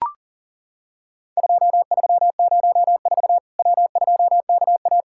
Morse code, heard as a keyed tone, about 40 wpm, 700 Hz. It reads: 2304 W2KA
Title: Worked All Europe DX Contest - QTC Exchange - 40wpm